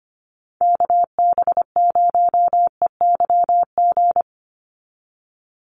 Morse code, heard as a keyed tone, about 25 wpm, 700 Hz. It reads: KB0EYG
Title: USA Call Signs 25wpm